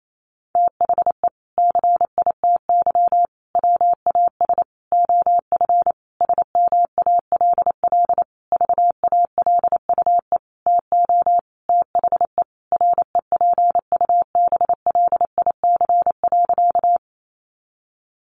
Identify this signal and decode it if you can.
Morse code, heard as a keyed tone, about 28 wpm, 700 Hz. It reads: THE CITY WAS OF SMALL VALUE TO THE REPUBLIC.
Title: Sentences from Top 1500 Words 28wpm